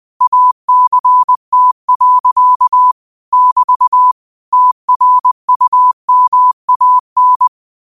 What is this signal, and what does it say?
Morse code, heard as a keyed tone, about 20 wpm, 1000 Hz. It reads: ACT. <BT> TRUMAN
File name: qotdfiles / 20wpm